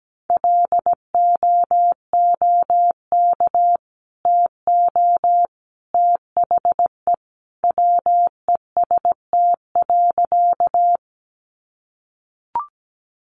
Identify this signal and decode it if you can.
Morse code, heard as a keyed tone, about 17 wpm, 700 Hz. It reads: LOOK TO THE WEST.
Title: Sentences from Top 400 Words 17wpm